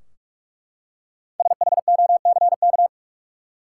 Morse code, heard as a keyed tone, about 43 wpm, 700 Hz. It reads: SHOCK
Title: Five Letter Words - 45wpm